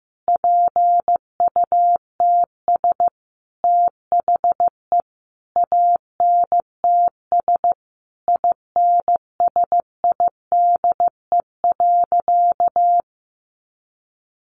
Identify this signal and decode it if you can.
Morse code, heard as a keyed tone, about 15 wpm, 700 Hz. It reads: PUTS THE ANTS INSIDE.
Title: Additional Easy Phrases 15wpm